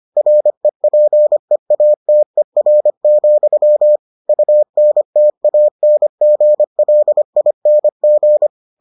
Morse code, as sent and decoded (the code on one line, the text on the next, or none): .-. .--. .- - . .-. --..-- / ..- -. - .- -. --. .-.. .. -. --.
REPEATER, UNTANGLING